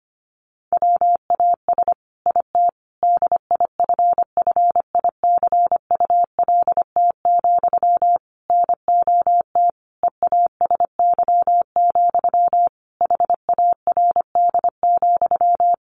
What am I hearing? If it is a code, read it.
Morse code, heard as a keyed tone, about 25 wpm, 700 Hz. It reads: WAS IT DIFFICULT, NOT EASY, HARD,
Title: Sentences from Top 1200 Words 25wpm